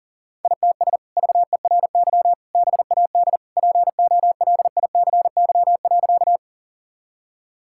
Morse code, heard as a keyed tone, about 40 wpm, 700 Hz. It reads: ITS VERY BAD POLICY.
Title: Sentences from Top 1000 Words - Review Part 5 - 40wpm